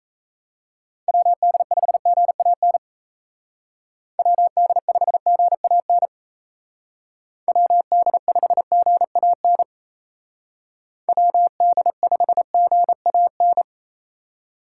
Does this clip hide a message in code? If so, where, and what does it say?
Yes: Morse code from 1.1 to 13.6 s, heard as a keyed tone, about 37 wpm, 700 Hz. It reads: WD5GAN WD5GAN WD5GAN WD5GAN